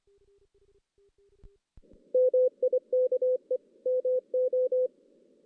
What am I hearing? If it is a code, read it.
Morse code, heard as a keyed tone, about 25 wpm, 440 Hz. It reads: MIKE MO
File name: 2016 JAN CW NAQP / 2016 Jan naqp k5ot - 0-5